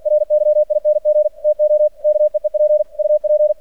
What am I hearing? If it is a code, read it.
Morse code, heard as a keyed tone, about 49 wpm, 600 Hz. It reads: RYING TO KEEP UP
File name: archived / CRSnet